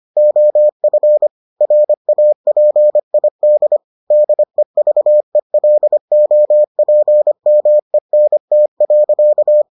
Morse code, heard as a keyed tone, about 25 wpm, 600 Hz. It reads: OF RAPID DEVELOPMENT.